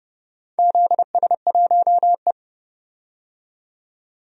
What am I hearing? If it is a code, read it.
Morse code, heard as a keyed tone, about 30 wpm, 700 Hz. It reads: ZS1E
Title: DXCC-100 Call Signs - 4 Characters - 30wpm